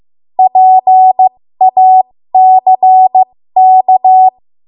Morse code, heard as a keyed tone, about 15 wpm, spaced as for 13 wpm, 750 Hz. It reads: PACK
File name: Archive / 13 WPM / 13WPM